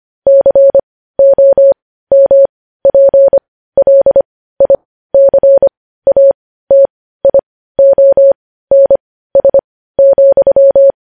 Morse code, heard as a keyed tone, about 25 wpm, spaced as for 15 wpm, 550 Hz. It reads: COMPLICATIONS,
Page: news-headlines-in-morse-code-at-15-wpm